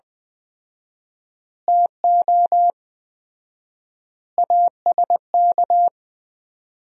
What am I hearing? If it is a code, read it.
Morse code, heard as a keyed tone, about 20 wpm, 700 Hz. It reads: TO ASK
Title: 4x Word-Spacing - Sentences from Top 200 Words - 20wpm